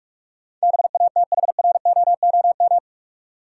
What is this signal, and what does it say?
Morse code, heard as a keyed tone, about 45 wpm, 700 Hz. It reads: BATHROOM
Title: Sets of 5 Words Encore - New Word Drill - 45wpm